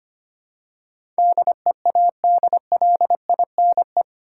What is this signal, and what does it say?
Morse code, heard as a keyed tone, about 25 wpm, 700 Hz. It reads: DEADLINE
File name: Eight Letter Words - 25wpm